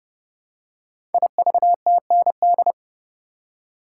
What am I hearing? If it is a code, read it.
Morse code, heard as a keyed tone, about 29 wpm, 700 Hz. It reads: IVTND
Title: Code Groups - 5 Letters - 30wpm